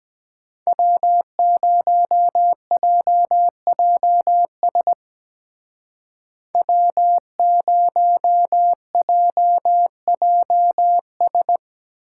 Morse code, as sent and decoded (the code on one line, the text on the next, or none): .-- ----- .--- .--- ... / .-- ----- .--- .--- ...
W0JJS W0JJS